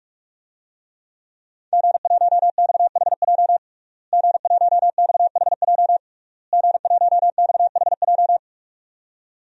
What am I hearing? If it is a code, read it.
Morse code, heard as a keyed tone, about 46 wpm, 700 Hz. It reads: G1XHJ G1XHJ G1XHJ
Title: DXCC-100 Call Signs - 5 Characters - Repeated 3x - 45wpm